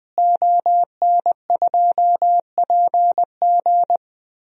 Morse code, heard as a keyed tone, about 21 wpm, 700 Hz. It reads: ON2PG
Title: DXCC-100 Call Signs - 5 Characters - Repeated 3x - 20wpm